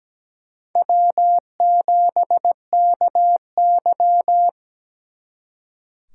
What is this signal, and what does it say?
Morse code, heard as a keyed tone, about 20 wpm, 700 Hz. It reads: W7KY
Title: Speed Racing - USA 1x2 Call Signs 17wpm